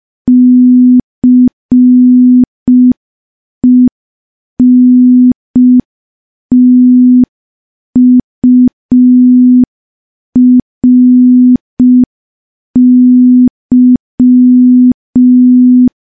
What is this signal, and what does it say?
Morse code, heard as a keyed tone, about 5 wpm, 250 Hz. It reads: CENTURY
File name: morse / morse-training-05WPM